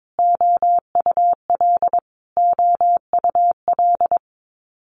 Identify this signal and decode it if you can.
Morse code, heard as a keyed tone, about 22 wpm, 700 Hz. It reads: OUL OUL
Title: Multi-letter Phonemes Repeated 3x 22wpm